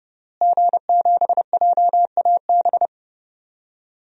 Morse code, heard as a keyed tone, about 30 wpm, 700 Hz. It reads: G7JAB
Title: DXCC-100 Call Signs - 5 Characters - Repeated 2x - 30wpm